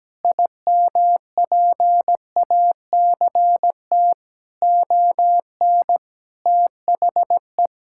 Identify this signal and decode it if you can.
Morse code, heard as a keyed tone, about 17 wpm, 700 Hz. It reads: IMPACT ON THE
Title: Sentences from Top 1000 Words - Review Part 6 - 17wpm